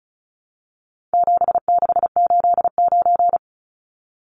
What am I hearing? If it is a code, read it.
Morse code, heard as a keyed tone, about 34 wpm, 700 Hz. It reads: 7689
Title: Numbers - 4-Digit 35wpm